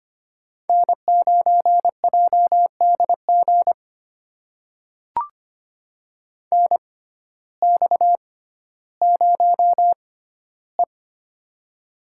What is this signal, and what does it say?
Morse code, heard as a keyed tone, about 25 wpm, spaced as for 14 wpm, 700 Hz. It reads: N9JDG NX0E
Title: USA Call Signs - 6x Character Spacing - 25wpm